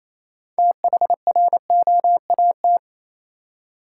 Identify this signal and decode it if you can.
Morse code, heard as a keyed tone, about 28 wpm, 700 Hz. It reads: THROAT
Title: Six Letter Words - 28wpm